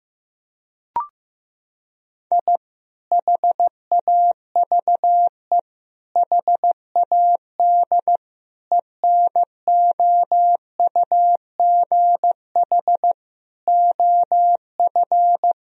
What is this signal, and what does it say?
Morse code, heard as a keyed tone, about 15 wpm, 700 Hz. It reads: I HAVE HAD ENOUGH OF
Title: Sentences from Top 1000 Words - Review Part 3 - 15wpm